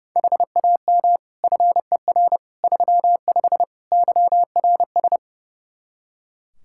Morse code, heard as a keyed tone, about 30 wpm, 700 Hz. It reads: HAM FER 35 YRS
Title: Practice QSO - Exchange 3 - More Info - 30wpm